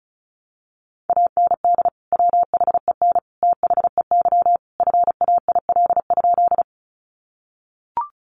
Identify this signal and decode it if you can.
Morse code, heard as a keyed tone, about 34 wpm, 700 Hz. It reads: AND WHEN THEY FAIL?
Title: Sentences from Top 1100 Words 35wpm